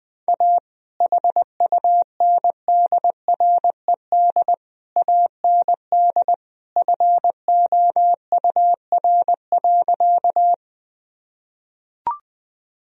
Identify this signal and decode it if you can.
Morse code, heard as a keyed tone, about 20 wpm, 700 Hz. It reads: A HUNDRED AND FOUR.
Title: Sentences from Top 1000 Words - Review Part 2 - 20wpm